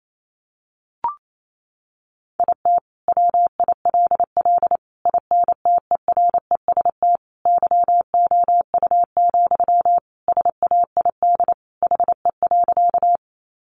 Morse code, heard as a keyed tone, about 28 wpm, 700 Hz. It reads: IT WILL INTEREST YOU, SAID HE.
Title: Sentences from Top 400 Words 28wpm